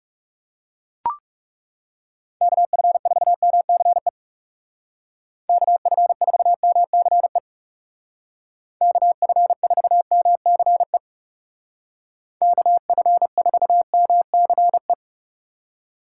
Morse code, heard as a keyed tone, about 40 wpm, 700 Hz. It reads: KF4MCE KF4MCE KF4MCE KF4MCE